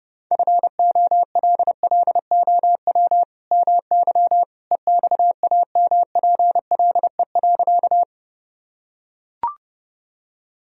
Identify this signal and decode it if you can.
Morse code, heard as a keyed tone, about 30 wpm, 700 Hz. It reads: FOLLOW MY EXAMPLE.
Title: Sentences from Top 1000 Words - Review Part 1 - 30wpm